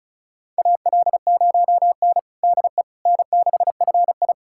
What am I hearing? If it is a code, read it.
Morse code, heard as a keyed tone, about 36 wpm, 700 Hz. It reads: AL0N DE N6FI
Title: USA Callsign DE Callsign - 4 Characters - 35wpm